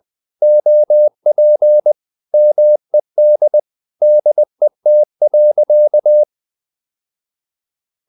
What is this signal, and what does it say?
Morse code, heard as a keyed tone, about 20 wpm, 600 Hz. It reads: OP MED DET.
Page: Ferie 20 wpm